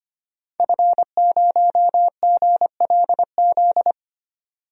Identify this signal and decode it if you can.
Morse code, heard as a keyed tone, about 25 wpm, 700 Hz. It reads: F0GLZ
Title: DXCC-100 Call Signs - 5 Characters - 25wpm